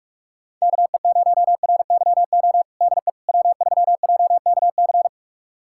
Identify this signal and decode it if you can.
Morse code, heard as a keyed tone, about 43 wpm, 700 Hz. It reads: KE0RYO DE W3JKC
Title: USA Callsign DE Callsign 45wpm